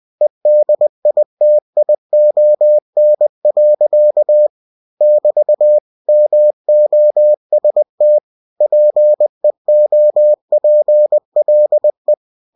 Morse code, heard as a keyed tone, about 20 wpm, 600 Hz. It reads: EDITION. <BT> MOST PEOPLE
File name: cw-books / winnie-the-pooh / winnie-20wpm-page-0122